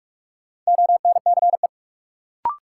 Morse code, heard as a keyed tone, about 41 wpm, 700 Hz. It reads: ONCE
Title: Sets of 4 Words - New Word Drill - 45wpm